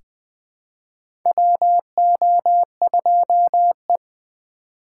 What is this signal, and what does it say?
Morse code, heard as a keyed tone, about 20 wpm, 700 Hz. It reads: WO2E
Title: K1USN SST QSO Practice - 20wpm